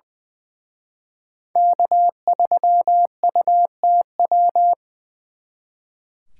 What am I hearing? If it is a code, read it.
Morse code, heard as a keyed tone, about 23 wpm, 700 Hz. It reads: K3UTW
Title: Speed Racing - USA 1x3 Call Signs 20wpm